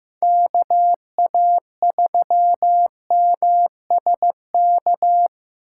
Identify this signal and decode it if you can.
Morse code, heard as a keyed tone, about 18 wpm, 700 Hz. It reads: KA3MSK
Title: Speed Racing - USA 2x3 Call Signs 15wpm